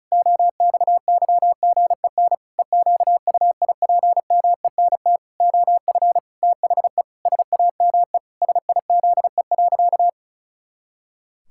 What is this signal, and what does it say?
Morse code, heard as a keyed tone, about 34 wpm, 700 Hz. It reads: OXYGEN EQUIPMENT OF THE SAME SIZE.
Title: Pangram Challenge v2 - 35wpm